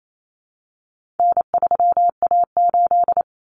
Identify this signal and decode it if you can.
Morse code, heard as a keyed tone, about 28 wpm, 700 Hz. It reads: N3A8
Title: Common Cut Numbers - 4-Digit 28wpm